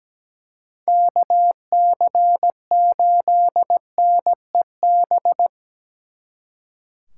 Morse code, heard as a keyed tone, about 20 wpm, 700 Hz. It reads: KC8NEB
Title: Speed Racing - USA 2x3 Call Signs 17wpm